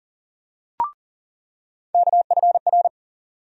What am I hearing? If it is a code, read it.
Morse code, heard as a keyed tone, about 39 wpm, 700 Hz. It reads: KFR
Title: USA Call Sign Postfixes - 3 Letters 40wpm